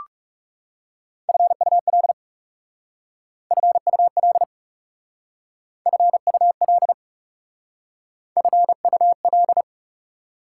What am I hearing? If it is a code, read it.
Morse code, heard as a keyed tone, about 38 wpm, 700 Hz. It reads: FUL FUL FUL FUL